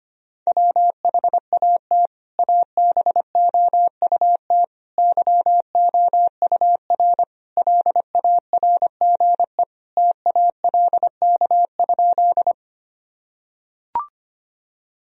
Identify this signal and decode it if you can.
Morse code, heard as a keyed tone, about 25 wpm, 700 Hz. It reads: WHAT ABOUT YOUR LARGE TALK?
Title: Sentences from Top 1000 Words - Review Part 1 - 25wpm